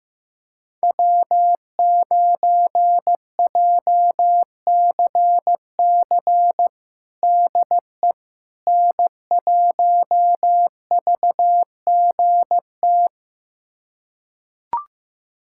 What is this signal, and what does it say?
Morse code, heard as a keyed tone, about 15 wpm, 700 Hz. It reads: W9JCC DE N1VGT
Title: USA Callsign DE Callsign - 5 Characters - 15wpm